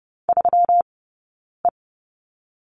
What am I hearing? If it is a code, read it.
Morse code, heard as a keyed tone, about 30 wpm, 700 Hz. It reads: 3 E
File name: Lesson 20d - Number 3 - Two Characters 30wpm